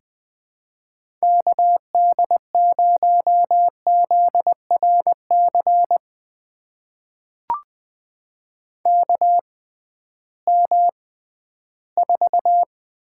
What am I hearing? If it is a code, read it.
Morse code, heard as a keyed tone, about 20 wpm, spaced as for 13 wpm, 700 Hz. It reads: KD0ZRC KM4
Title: USA Call Signs - 6 Characters - 6x Character Spacing - 20wpm